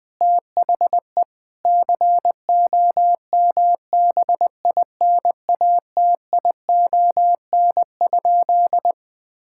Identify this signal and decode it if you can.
Morse code, heard as a keyed tone, about 21 wpm, 700 Hz. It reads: THE COMBINATION?
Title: Sentences from Top 1000 Words - Review Part 6 - 20wpm